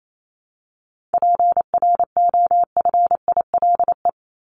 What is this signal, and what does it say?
Morse code, heard as a keyed tone, about 28 wpm, 700 Hz. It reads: PROFILE
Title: Top 1501-1750 Words 28wpm